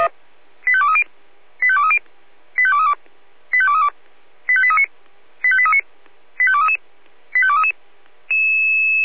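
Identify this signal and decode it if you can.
Five-tone selective call (ZVEI), heard as quick runs of five tones, tones of about 70 ms. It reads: 85230 85230 85232 85232 85839 85839 85233 85233
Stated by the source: Five-tone ZVEI selective call received by SDR.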